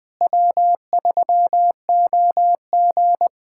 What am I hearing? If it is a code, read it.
Morse code, heard as a keyed tone, about 20 wpm, 700 Hz. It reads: W3OG
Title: USA Callsign DE Callsign - 4 Characters - 20wpm